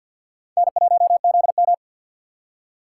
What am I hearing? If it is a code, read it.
Morse code, heard as a keyed tone, about 66 wpm, spaced as for 58 wpm, 700 Hz. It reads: N1ZM
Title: Speed Racing - USA 1x2 Call Signs 50wpm